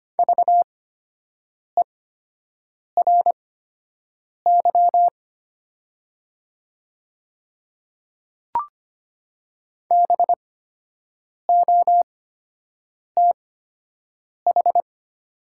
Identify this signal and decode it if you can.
Morse code, heard as a keyed tone, about 25 wpm, spaced as for 7 wpm, 700 Hz. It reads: VERY BOTH
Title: 8x Character Spacing - Top 300 Words - 25wpm